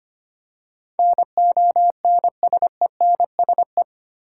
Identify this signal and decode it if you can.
Morse code, heard as a keyed tone, about 25 wpm, 700 Hz. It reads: NONSENSE
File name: 8 Letter Words - First 3 Letters Given - 25wpm